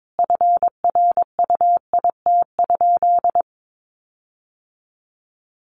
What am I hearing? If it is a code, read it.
Morse code, heard as a keyed tone, about 22 wpm, 700 Hz. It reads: FRUIT?
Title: Sentences from Top 1500 Words 22wpm